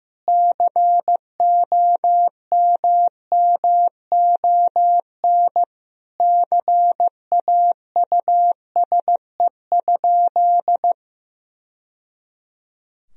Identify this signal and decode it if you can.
Morse code, heard as a keyed tone, about 15 wpm, 700 Hz. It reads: COMMON CAUSE?
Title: Sentences from Top 1000 Words - Review Part 3 - 15wpm